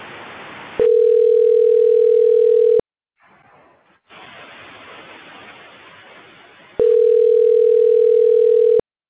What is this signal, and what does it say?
Ringback tone (North America), 440 + 480 Hz.